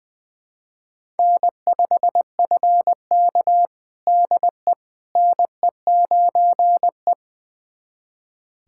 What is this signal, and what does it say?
Morse code, heard as a keyed tone, about 20 wpm, 700 Hz. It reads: N5FK DE NE9E
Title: USA Callsign DE Callsign - 4 Characters - 20wpm